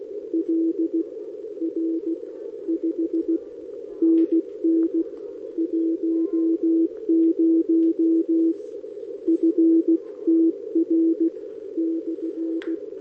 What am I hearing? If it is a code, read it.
Morse code, heard as a keyed tone, about 16 wpm, 330 Hz. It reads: L R 5 NN 10 FTR /
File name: OK QRP 2026 / OKQRP2026